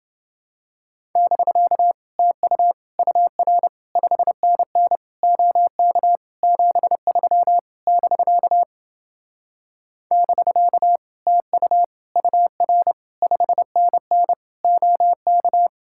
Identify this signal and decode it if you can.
Morse code, heard as a keyed tone, about 30 wpm, 700 Hz. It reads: <BK> TU UR 5NN OK 73 <BK> <BK> TU UR 5NN OK